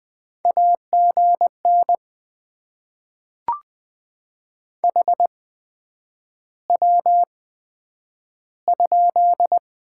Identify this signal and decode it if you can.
Morse code, heard as a keyed tone, about 20 wpm, spaced as for 9 wpm, 700 Hz. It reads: AGN HW?
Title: Top 20 QSO Elements - 8x Character Spacing - 20wpm